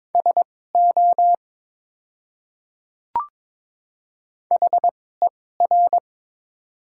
Morse code, heard as a keyed tone, about 22 wpm, spaced as for 16 wpm, 690 Hz. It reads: SO HER
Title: 2x Character Spacing - Top 100 Words - 22wpm